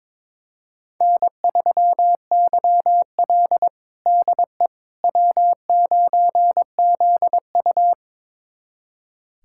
Morse code, heard as a keyed tone, about 22 wpm, 700 Hz. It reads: N3YL DE W9ZU
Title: USA Callsign DE Callsign - 4 Characters - 22wpm